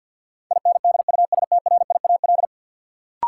Morse code, heard as a keyed tone, about 51 wpm, 700 Hz. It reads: INDUSTRIAL
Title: Sets of 5 Words - New Word Drill - 50wpm